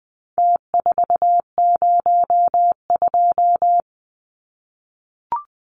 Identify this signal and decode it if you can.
Morse code, heard as a keyed tone, about 20 wpm, 700 Hz. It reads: T402
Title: Numbers - Contest Style - 4-Digit 20wpm